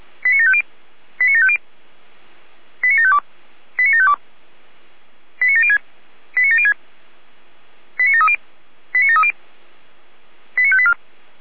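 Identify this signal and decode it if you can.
Five-tone selective call (ZVEI), heard as quick runs of five tones, tones of about 80 ms. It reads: 89750 89750 89752 89752 89706 89706 89730 89730 89584
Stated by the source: ZVEI five-tone selective call received on SDR.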